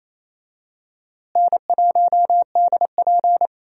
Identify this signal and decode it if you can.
Morse code, heard as a keyed tone, about 28 wpm, 700 Hz. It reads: N1DP
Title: USA Call Signs - 4 Characters - Repeated 2x - 28wpm